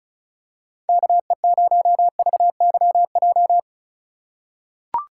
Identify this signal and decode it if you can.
Morse code, heard as a keyed tone, about 36 wpm, 700 Hz. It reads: KE0VYJ
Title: USA Call Signs - 4x Character Spacing - 35wpm